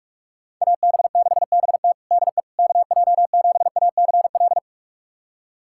Morse code, heard as a keyed tone, about 46 wpm, 700 Hz. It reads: AB6BT DE KJ7ACL
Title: USA Callsign DE Callsign 45wpm